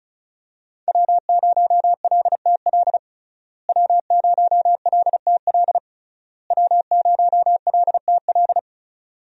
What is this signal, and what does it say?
Morse code, heard as a keyed tone, about 36 wpm, 700 Hz. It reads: W0LTL W0LTL W0LTL
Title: USA Call Signs - 5 Characters - Repeated 3x - 35wpm